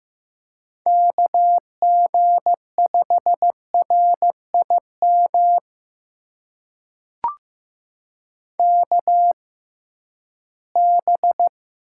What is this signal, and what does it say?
Morse code, heard as a keyed tone, about 15 wpm, spaced as for 11 wpm, 700 Hz. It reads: KG5RIM KB
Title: USA Call Signs - 6 Characters - 6x Character Spacing - 15wpm